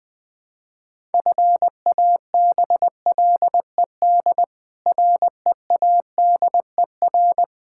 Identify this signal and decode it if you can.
Morse code, heard as a keyed tone, about 20 wpm, 700 Hz. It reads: FABLED READER
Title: Pangram Challenge v2 - 20wpm